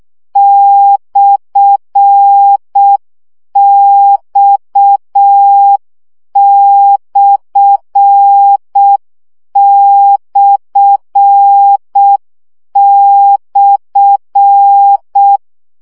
Morse code, heard as a keyed tone, about 6 wpm, 800 Hz. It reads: /X///
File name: CW gyakorlas x per 5 baud